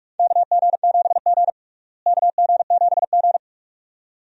Morse code, heard as a keyed tone, about 46 wpm, 700 Hz. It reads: KG7G KG7G
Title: USA Call Signs - 4 Characters - Repeated 2x - 45wpm